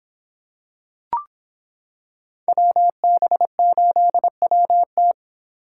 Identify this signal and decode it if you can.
Morse code, heard as a keyed tone, about 25 wpm, 700 Hz. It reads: WB8WT